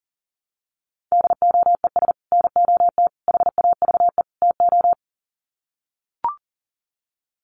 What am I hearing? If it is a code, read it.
Morse code, heard as a keyed tone, about 40 wpm, 700 Hz. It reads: DOES NOT HAVE TO
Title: Sets of 4 Words 40wpm